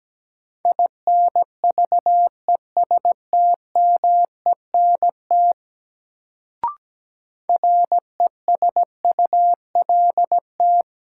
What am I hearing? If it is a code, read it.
Morse code, heard as a keyed tone, about 17 wpm, 700 Hz. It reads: INVESTMENT RESULT
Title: Sentences from Top 700 Words - New Word Drill - 17wpm